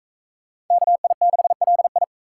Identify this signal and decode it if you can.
Morse code, heard as a keyed tone, about 41 wpm, 700 Hz. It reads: KI6LI